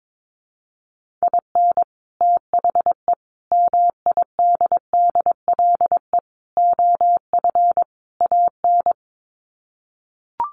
Morse code, heard as a keyed tone, about 22 wpm, 700 Hz. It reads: IN THE MIDDLE OF AN
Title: Sets of 5 Words Encore2 22wpm